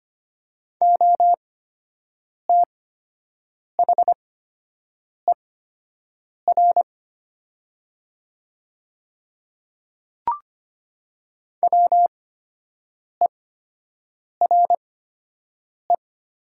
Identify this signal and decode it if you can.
Morse code, heard as a keyed tone, about 25 wpm, spaced as for 7 wpm, 700 Hz. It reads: OTHER WERE
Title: 8x Character Spacing - Top 100 Words - 25wpm